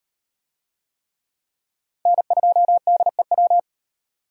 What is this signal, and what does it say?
Morse code, heard as a keyed tone, about 38 wpm, 700 Hz. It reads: N2DEW